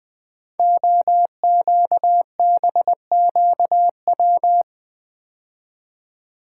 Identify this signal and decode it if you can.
Morse code, heard as a keyed tone, about 20 wpm, 700 Hz. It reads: OQBQW
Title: Code Groups - 5 Letters - 20wpm